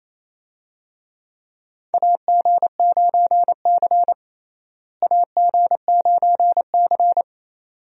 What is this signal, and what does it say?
Morse code, heard as a keyed tone, about 28 wpm, 700 Hz. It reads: AG9C AG9C
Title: USA Call Signs - 4 Characters - Repeated 2x - 28wpm